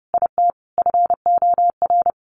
Morse code, heard as a keyed tone, about 30 wpm, 700 Hz. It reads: IT FOR
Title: A Sound Of Thunder 30wpm